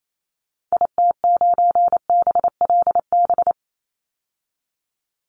Morse code, heard as a keyed tone, about 28 wpm, 690 Hz. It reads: IT9BLB
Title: Call Signs Repeated 3x - 28wpm